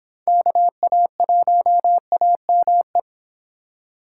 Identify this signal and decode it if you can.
Morse code, heard as a keyed tone, about 26 wpm, 700 Hz. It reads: KA1AME